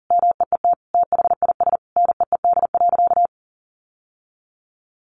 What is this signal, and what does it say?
Morse code, heard as a keyed tone, about 37 wpm, 700 Hz. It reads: MEET THIS NEED.
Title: Sentences from Top 1100 Words 40wpm